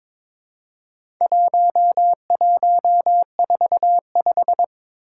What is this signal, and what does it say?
Morse code, heard as a keyed tone, about 22 wpm, 700 Hz. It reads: 1145
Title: Four Challenging Letter-Number Combinations 22wpm